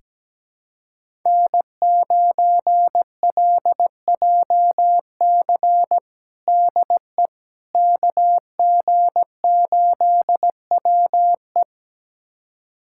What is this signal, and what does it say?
Morse code, heard as a keyed tone, about 17 wpm, 700 Hz. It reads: N9LJC DE KG8WE
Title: USA Callsign DE Callsign - 5 Characters - 17wpm